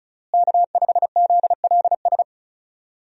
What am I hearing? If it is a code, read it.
Morse code, heard as a keyed tone, about 36 wpm, 700 Hz. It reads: K5ZLS
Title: USA Call Signs - 5 Characters - Repeated 3x - 35wpm